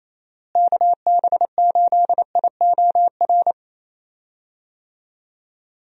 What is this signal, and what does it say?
Morse code, heard as a keyed tone, about 28 wpm, 700 Hz. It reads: KB8IOR
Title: USA Call Signs - 6 Characters - Repeated 3x - 28wpm